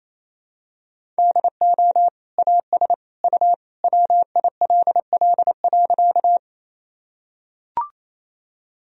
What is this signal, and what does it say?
Morse code, heard as a keyed tone, about 28 wpm, 700 Hz. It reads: DO AS U WILL.
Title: Sentences from Top 100 Words and Abbreviations 28wpm